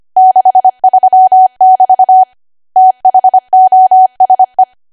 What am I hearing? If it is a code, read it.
Morse code, heard as a keyed tone, about 25 wpm, 750 Hz. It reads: 63 <BT> THOSE
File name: Archive / 25 WPM / 25WPM